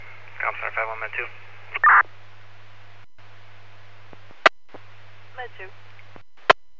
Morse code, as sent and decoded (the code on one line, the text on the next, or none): none